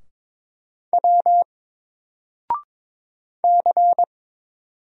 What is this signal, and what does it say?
Morse code, heard as a keyed tone, about 22 wpm, 700 Hz. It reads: W C
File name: Single-Letter-Number-22wpm